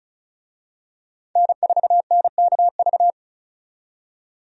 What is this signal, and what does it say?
Morse code, heard as a keyed tone, about 36 wpm, 700 Hz. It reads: N4NKV
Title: USA Call Signs - 5 Characters - 35wpm